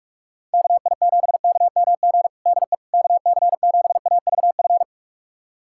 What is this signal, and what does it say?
Morse code, heard as a keyed tone, about 46 wpm, 700 Hz. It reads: KI7KMG DE KC7AVF
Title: USA Callsign DE Callsign - 6 Characters - 45wpm